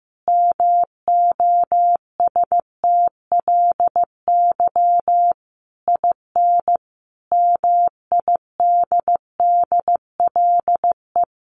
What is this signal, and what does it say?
Morse code, heard as a keyed tone, about 15 wpm, 700 Hz. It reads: MOSTLY IN MIDDLE
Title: Sentences from Top 1300 Words 15wpm